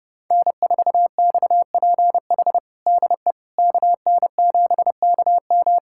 Morse code, heard as a keyed tone, about 30 wpm, 700 Hz. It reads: N4XPH DE KN7KM
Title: USA Callsign DE Callsign - 5 Characters - 30wpm